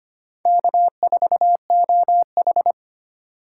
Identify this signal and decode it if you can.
Morse code, heard as a keyed tone, about 25 wpm, 700 Hz. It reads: K4OH
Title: USA Call Signs - 4 Characters - 3x Character Spacing - 25wpm